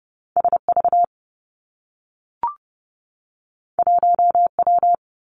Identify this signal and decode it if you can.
Morse code, heard as a keyed tone, about 30 wpm, 700 Hz. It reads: S V 1 W
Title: Two Challenging Letter-Number Combinations 30wpm